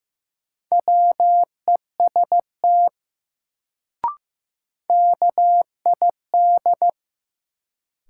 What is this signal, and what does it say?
Morse code, heard as a keyed tone, about 16 wpm, 700 Hz. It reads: WEST KID
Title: Sets of 5 Words - New Word Drill - 15wpm